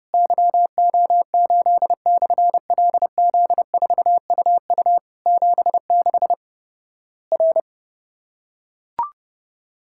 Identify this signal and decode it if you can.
Morse code, heard as a keyed tone, about 32 wpm, 700 Hz. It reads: YO8/LZ4UU 76 R
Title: Worked All Europe DX Contest - QTC Exchange - 30wpm